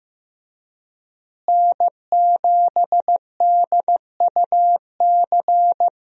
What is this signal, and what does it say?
Morse code, heard as a keyed tone, about 15 wpm, 700 Hz. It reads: N7DUC
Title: USA Call Signs - 5 Characters - Repeated 3x - 15wpm